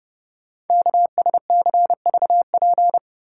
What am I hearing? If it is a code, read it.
Morse code, heard as a keyed tone, about 30 wpm, 690 Hz. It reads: KSCVP
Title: Code Groups - 5 Letters - 30wpm